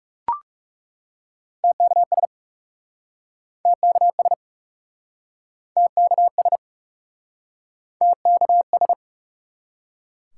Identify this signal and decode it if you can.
Morse code, heard as a keyed tone, about 38 wpm, 700 Hz. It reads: TKS TKS TKS TKS